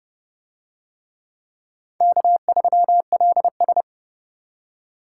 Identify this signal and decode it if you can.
Morse code, heard as a keyed tone, about 26 wpm, 700 Hz. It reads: K3LS